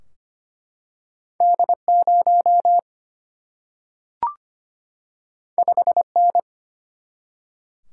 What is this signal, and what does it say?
Morse code, heard as a keyed tone, about 25 wpm, 700 Hz. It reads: D0 5N
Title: Single-Letter Single-Number 25wpm